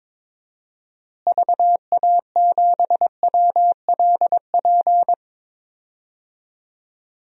Morse code, heard as a keyed tone, about 22 wpm, 700 Hz. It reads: VA7WLP
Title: DXCC-100 Call Signs - 6 Characters - Repeated 2x - 22wpm